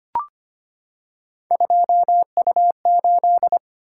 Morse code, heard as a keyed tone, about 25 wpm, 700 Hz. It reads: 2U8
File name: Three Letter-Number Combinations - 25wpm